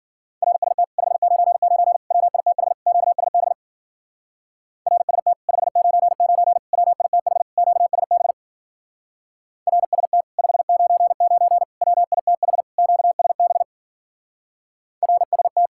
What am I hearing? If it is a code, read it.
Morse code, heard as a keyed tone, about 52 wpm, 700 Hz. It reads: RST 599 WITH QSB RST 599 WITH QSB RST 599 WITH QSB RST